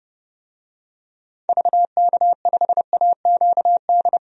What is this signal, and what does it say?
Morse code, heard as a keyed tone, about 30 wpm, 700 Hz. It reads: VK5AQD
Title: DXCC-100 Call Signs - 6 Characters - 30wpm